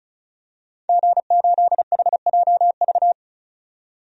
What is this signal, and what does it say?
Morse code, heard as a keyed tone, about 36 wpm, 700 Hz. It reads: G8HJV
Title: DXCC-100 Call Signs - 5 Characters - 35wpm